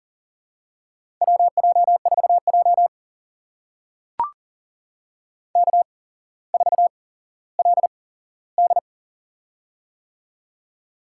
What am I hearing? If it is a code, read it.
Morse code, heard as a keyed tone, about 40 wpm, spaced as for 17 wpm, 700 Hz. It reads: WJ4J K4LD
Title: USA Call Signs - 4 Characters - 8x Character Spacing - 40wpm